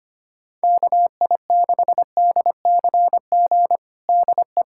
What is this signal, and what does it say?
Morse code, heard as a keyed tone, about 25 wpm, 700 Hz. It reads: KI6DCG DE
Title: USA Callsign DE Callsign - 6 Characters - 25wpm